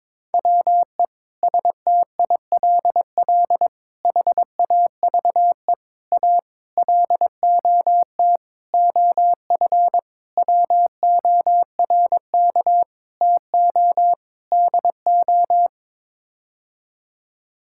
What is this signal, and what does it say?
Morse code, heard as a keyed tone, about 22 wpm, 700 Hz. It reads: WE STILL HAVE A LOT OF WORK TO DO
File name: Sets of 9 Words - 22wpm